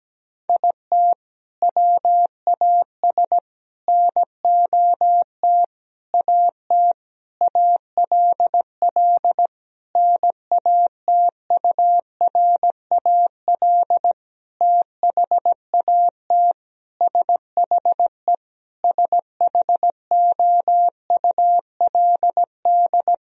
Morse code, heard as a keyed tone, about 17 wpm, 700 Hz. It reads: IT WAS NOT AT ALL NATURAL THAT SHE SHOULD
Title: Sentences from Top 1000 Words - Review Part 4 - 17wpm